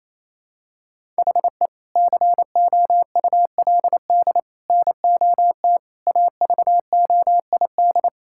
Morse code, heard as a keyed tone, about 28 wpm, 700 Hz. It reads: HE COULD NOT AVOID
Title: Sentences from Top 1000 Words - Review Part 6 - 28wpm